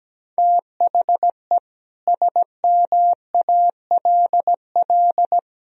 Morse code, heard as a keyed tone, about 17 wpm, 700 Hz. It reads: THE SMALL
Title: Sentences from Top 1000 Words - Review Part 1 - 17wpm